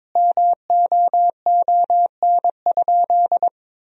Morse code, heard as a keyed tone, about 22 wpm, 700 Hz. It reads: MOON?
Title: Sentences from Top 1000 Words - Review Part 3 - 22wpm